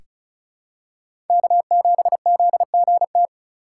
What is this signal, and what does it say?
Morse code, heard as a keyed tone, about 36 wpm, 700 Hz. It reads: K7ZGT
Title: USA Call Signs - 5 Characters - 3x Character Spacing - 35wpm